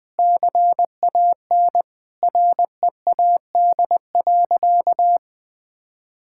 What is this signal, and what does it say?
Morse code, heard as a keyed tone, about 21 wpm, 700 Hz. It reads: CAN READ.
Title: Sentences with 8-Word Phrases - 20wpm